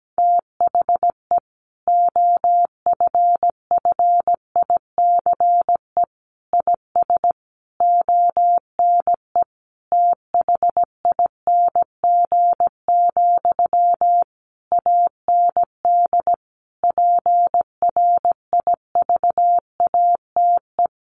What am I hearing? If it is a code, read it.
Morse code, heard as a keyed tone, about 17 wpm, 700 Hz. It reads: THE OFFICE IS ONE THING, AND PRIVATE
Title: Sentences from Top 1000 Words 17wpm